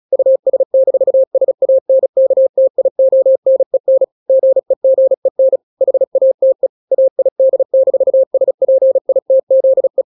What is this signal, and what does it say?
Morse code, heard as a keyed tone, about 36 wpm, 530 Hz. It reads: US-SANKTIONEN GEGEN HATE AID-SPITZE
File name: daily / 35wpm